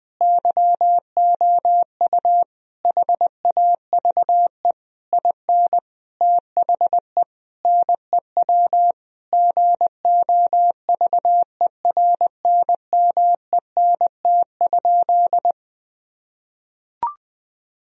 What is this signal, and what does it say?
Morse code, heard as a keyed tone, about 21 wpm, 700 Hz. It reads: YOU HAVE IN THE NEW GOVERNMENT?
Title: Sentences from Top 1000 Words - Review Part 3 - 20wpm